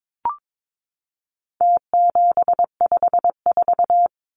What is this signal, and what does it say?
Morse code, heard as a keyed tone, about 22 wpm, 700 Hz. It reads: T754
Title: Numbers - Contest Style - 4-Digit 22wpm